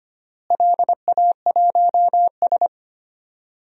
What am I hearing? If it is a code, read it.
Morse code, heard as a keyed tone, about 25 wpm, 700 Hz. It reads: LA1S
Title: DXCC-100 Call Signs - 4 Characters - Repeated 2x - 25wpm